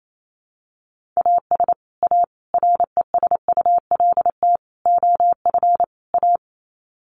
Morse code, heard as a keyed tone, about 28 wpm, 700 Hz. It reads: AS A RESULT OF A
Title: Sets of 5 Words Encore 28wpm